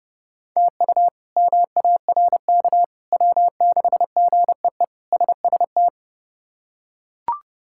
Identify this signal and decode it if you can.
Morse code, heard as a keyed tone, about 32 wpm, 700 Hz. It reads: TU MARK W6GEE SST
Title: K1USN SST QSO Practice - 30wpm